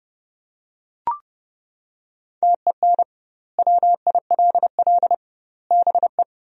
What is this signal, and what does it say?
Morse code, heard as a keyed tone, about 32 wpm, 700 Hz. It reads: TEN WILL BE
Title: 2x Word-Spacing - Sentences from Top 400 Words - 30wpm